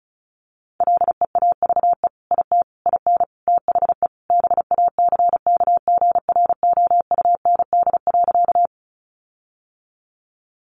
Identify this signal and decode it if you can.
Morse code, heard as a keyed tone, about 34 wpm, 700 Hz. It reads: LEAVE IT IN THE BACKGROUND.
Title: Sentences from Top 1300 Words 35wpm